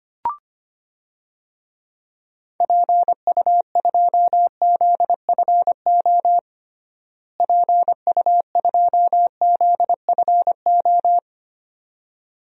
Morse code, heard as a keyed tone, about 25 wpm, 700 Hz. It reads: PU2ZFO PU2ZFO
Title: DXCC-100 Call Signs - 6 Characters - Repeated 2x - 25wpm